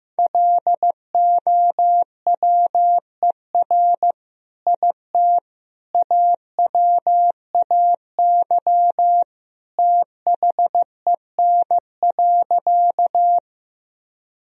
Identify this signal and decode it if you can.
Morse code, heard as a keyed tone, about 15 wpm, 700 Hz. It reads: LOWER IT AWAY THEN.
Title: Sentences from Top 1000 Words - Review Part 4 - 15wpm